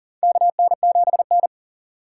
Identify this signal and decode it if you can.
Morse code, heard as a keyed tone, about 40 wpm, 700 Hz. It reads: KN7N
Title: USA Call Signs - 4 Characters - 6x Character Spacing - 40wpm